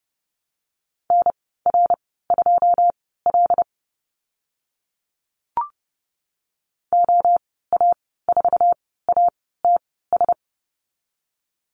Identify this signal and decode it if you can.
Morse code, heard as a keyed tone, about 30 wpm, spaced as for 17 wpm, 700 Hz. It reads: NR2L OA4ATS
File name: Lesson 14g - Number 2 - Call Signs 30wpm